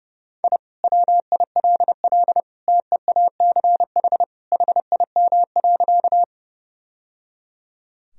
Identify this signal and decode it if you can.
Morse code, heard as a keyed tone, about 30 wpm, 700 Hz. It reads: I WILL TEACH HIM.
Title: Sentences from Top 1000 Words - Review Part 2 - 30wpm